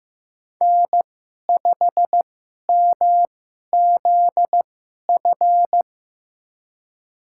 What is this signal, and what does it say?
Morse code, heard as a keyed tone, about 15 wpm, spaced as for 11 wpm, 700 Hz. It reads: N5MZF
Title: USA Call Signs - 5 Characters - 2x Character Spacing - 15wpm